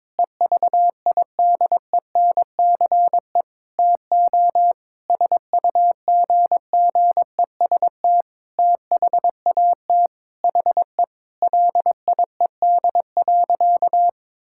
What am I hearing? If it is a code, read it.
Morse code, heard as a keyed tone, about 22 wpm, 700 Hz. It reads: EVIDENCE TO SUGGEST THAT HE LIED.
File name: Sentences with 7-Word Phrases - 22wpm